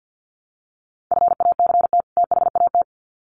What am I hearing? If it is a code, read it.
Morse code, heard as a keyed tone, about 50 wpm, 700 Hz. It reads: FACT THAT
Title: Sets of 2 Words 50wpm